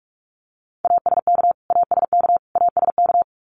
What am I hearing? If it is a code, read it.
Morse code, heard as a keyed tone, about 44 wpm, 700 Hz. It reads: ASK ASK ASK
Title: Top 200 Words Repeated 45wpm